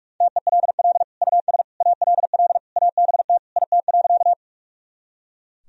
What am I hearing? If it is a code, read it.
Morse code, heard as a keyed tone, about 42 wpm, 700 Hz. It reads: TELL US ALL ABT IT.
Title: Sentences from Top 200 Words and Abbreviations 45wpm